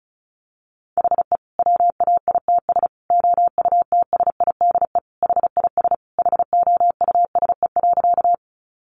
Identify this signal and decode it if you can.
Morse code, heard as a keyed tone, about 34 wpm, 700 Hz. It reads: HE WAITS OUTSIDE HIS HOUSE.
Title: Additional Easy Phrases 35wpm